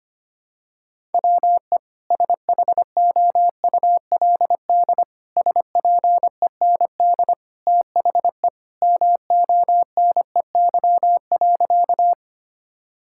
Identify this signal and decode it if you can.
Morse code, heard as a keyed tone, about 25 wpm, 700 Hz. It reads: WE SHOULD SPEND THE MONEY.
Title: Sentences from Top 1000 Words - Review Part 5 - 25wpm